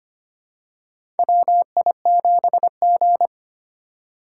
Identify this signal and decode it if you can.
Morse code, heard as a keyed tone, about 25 wpm, 700 Hz. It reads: WI7G
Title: USA Call Signs - 4 Characters - 2x Character Spacing - 25wpm